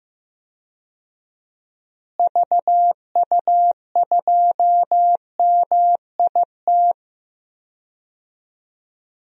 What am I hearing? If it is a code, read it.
Morse code, heard as a keyed tone, about 15 wpm, 700 Hz. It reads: VU2MIT
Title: DXCC-100 Call Signs - 6 Characters - Repeated 3x - 15wpm